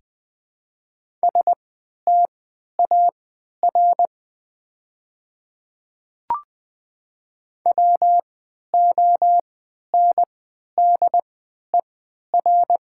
Morse code, heard as a keyed tone, about 20 wpm, spaced as for 11 wpm, 700 Hz. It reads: STAR WONDER
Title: 3x Character Spacing - Top 500 Words - 20wpm